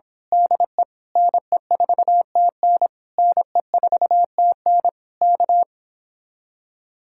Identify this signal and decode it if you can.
Morse code, heard as a keyed tone, about 26 wpm, 700 Hz. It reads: DE NE4TN NE4TN K
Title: Speed-Racing Common QSO Phrases Encore - 22wpm